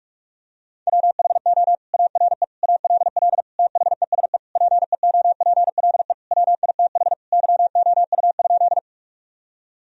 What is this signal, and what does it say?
Morse code, heard as a keyed tone, about 46 wpm, 700 Hz. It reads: WHO ARE ALL THESE PEOPLE WITH YOU?
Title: Sentences from Top 1000 Words - Review Part 1 - 45wpm